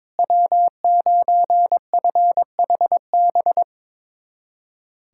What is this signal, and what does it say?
Morse code, heard as a keyed tone, about 22 wpm, 700 Hz. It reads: W9FHB
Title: USA Callsign DE Callsign - 5 Characters - 22wpm